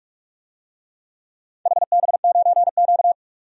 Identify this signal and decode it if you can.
Morse code, heard as a keyed tone, about 46 wpm, 700 Hz. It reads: HB9Q
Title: DXCC-100 Call Signs - 4 Characters - 45wpm